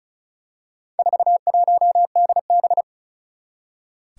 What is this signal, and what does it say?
Morse code, heard as a keyed tone, about 36 wpm, 700 Hz. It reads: 41DB
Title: Four Challenging Letter-Number Combinations 35wpm